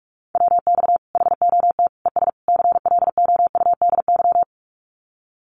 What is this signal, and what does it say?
Morse code, heard as a keyed tone, about 44 wpm, 700 Hz. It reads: WX HOT ES CLOUDY
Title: Common QSO Phrases 45wpm